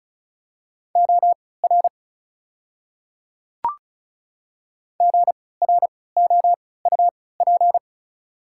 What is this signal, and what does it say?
Morse code, heard as a keyed tone, about 34 wpm, spaced as for 20 wpm, 700 Hz. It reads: OR GROUP
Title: 3x Character Spacing - Top 500 Words - 35wpm